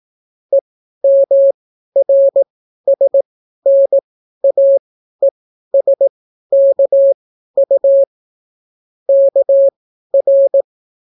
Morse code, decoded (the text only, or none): EMRSNAESKU KR